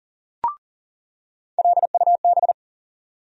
Morse code, heard as a keyed tone, about 40 wpm, 700 Hz. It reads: LUB
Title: USA Call Sign Postfixes - 3 Letters 40wpm